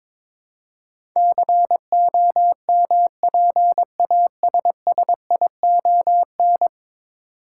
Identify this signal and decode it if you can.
Morse code, heard as a keyed tone, about 22 wpm, 700 Hz. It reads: COMPASSION
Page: morse-code-ninja-practice-22wpm